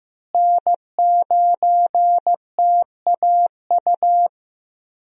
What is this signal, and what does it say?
Morse code, heard as a keyed tone, about 16 wpm, 700 Hz. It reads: N9TAU
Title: USA Call Signs - 5 Characters - Repeated 2x - 15wpm